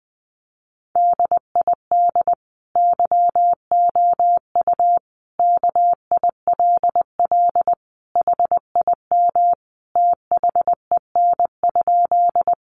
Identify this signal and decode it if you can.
Morse code, heard as a keyed tone, about 20 wpm, 700 Hz. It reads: DID YOU KILL HIM THEN?
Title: Sentences from Top 1400 Words 20wpm